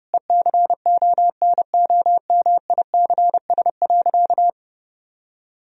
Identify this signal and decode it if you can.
Morse code, heard as a keyed tone, about 30 wpm, 700 Hz. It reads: ECONOMICS.
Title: Sentences from Top 1000 Words - Review Part 3 - 30wpm